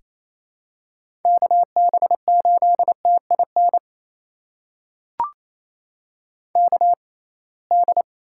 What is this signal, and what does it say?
Morse code, heard as a keyed tone, about 28 wpm, spaced as for 21 wpm, 700 Hz. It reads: KB8TIN KD
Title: USA Call Signs - 6 Characters - 6x Character Spacing - 28wpm